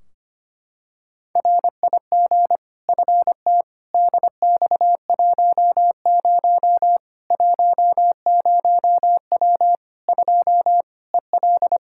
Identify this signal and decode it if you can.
Morse code, heard as a keyed tone, about 25 wpm, 700 Hz. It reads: RIG FT DX10 10W 2 EL
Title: K1USN SST QSO Practice 25wpm